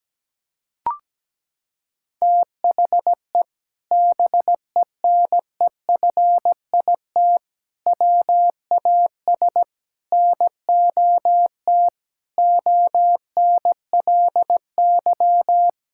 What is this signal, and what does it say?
Morse code, heard as a keyed tone, about 17 wpm, 700 Hz. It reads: THE BENEFIT WAS NOT ONLY
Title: Sentences from Top 1000 Words - Review Part 6 - 17wpm